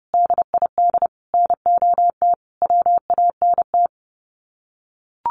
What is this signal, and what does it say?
Morse code, heard as a keyed tone, about 30 wpm, 700 Hz. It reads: DID NOT WANT
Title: Sets of 5 Words 30wpm